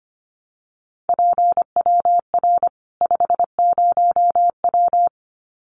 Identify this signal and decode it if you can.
Morse code, heard as a keyed tone, about 25 wpm, 700 Hz. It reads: PWR 50W
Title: Common QSO Phrases 25wpm